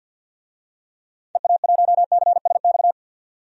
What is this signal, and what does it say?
Morse code, heard as a keyed tone, about 52 wpm, 700 Hz. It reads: EA1CSX
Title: DXCC-100 Call Signs - 6 Characters - 50wpm